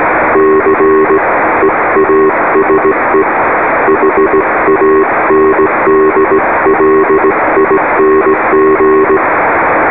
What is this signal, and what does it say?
Morse code, heard as a keyed tone, about 18 wpm, 390 Hz, with radio noise. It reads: C EASE HANDLING